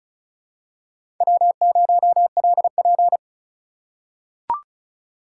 Morse code, heard as a keyed tone, about 34 wpm, 700 Hz. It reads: W0LP
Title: USA Call Signs - 4 Characters - 3x Character Spacing - 35wpm